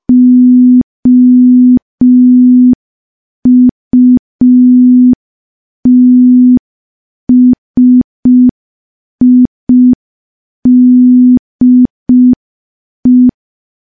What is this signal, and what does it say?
Morse code, heard as a keyed tone, about 5 wpm, 250 Hz. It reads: OUTSIDE
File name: morse / morse-training-05WPM